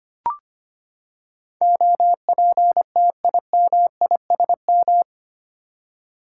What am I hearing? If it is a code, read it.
Morse code, heard as a keyed tone, about 25 wpm, 700 Hz. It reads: OPTIMISM
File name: Eight Letter Words - 25wpm